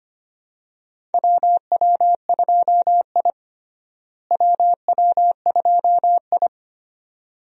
Morse code, heard as a keyed tone, about 25 wpm, 700 Hz. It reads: WW2I WW2I
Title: USA Call Signs - 4 Characters - Repeated 2x - 25wpm